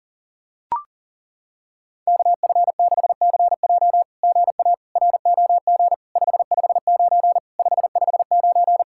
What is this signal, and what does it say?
Morse code, heard as a keyed tone, about 40 wpm, 700 Hz. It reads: KF6CJ GA ROG 559 559
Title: SOTA QSO Practice 40wpm